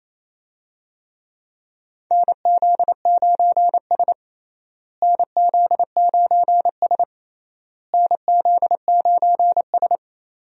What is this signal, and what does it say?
Morse code, heard as a keyed tone, about 28 wpm, 700 Hz. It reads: NZ9S NZ9S NZ9S
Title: USA Call Signs - 4 Characters - Repeated 3x - 28wpm